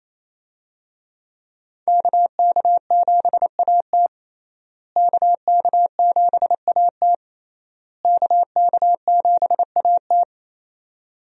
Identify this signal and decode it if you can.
Morse code, heard as a keyed tone, about 27 wpm, 700 Hz. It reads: KK7AT KK7AT KK7AT
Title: USA Call Signs - 5 Characters - Repeated 3x - 28wpm